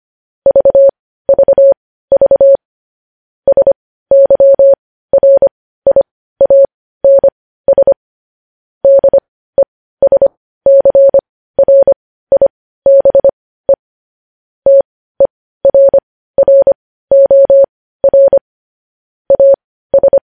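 Morse code, heard as a keyed tone, about 25 wpm, spaced as for 15 wpm, 550 Hz. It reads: VVV SYRIANS DESCRIBE TERROR AS
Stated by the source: News headlines in Morse code at 15 words per minute, characters sent at 25 WPM.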